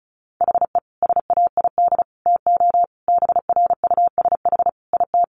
Morse code, heard as a keyed tone, about 34 wpm, 700 Hz. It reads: HE SAID TO BRUSH IT
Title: Sentences from Top 1100 Words 35wpm